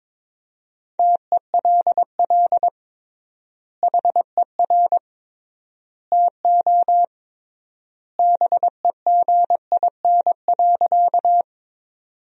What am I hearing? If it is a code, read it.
Morse code, heard as a keyed tone, about 22 wpm, 700 Hz. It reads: TELL HER TO BEGIN.
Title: 3x Word-Spacing - Sentences from Top 300 Words - 22wpm